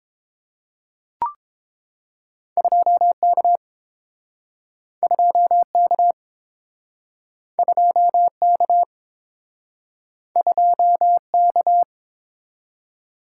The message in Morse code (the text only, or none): 2K 2K 2K 2K